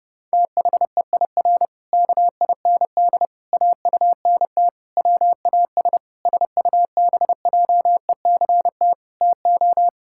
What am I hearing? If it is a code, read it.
Morse code, heard as a keyed tone, about 30 wpm, 700 Hz. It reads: THEIR KIND AUNT WAS SUBJECT TO
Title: Pangram Challenge v2 - 30wpm